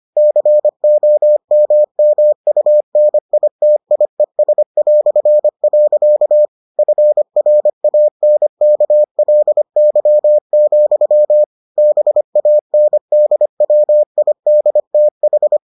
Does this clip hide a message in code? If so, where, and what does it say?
Yes: Morse code from 0.2 to 15.6 s, heard as a keyed tone, about 25 wpm, 600 Hz. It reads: COMMUNITIES". FRANKLY, BANDWIDTH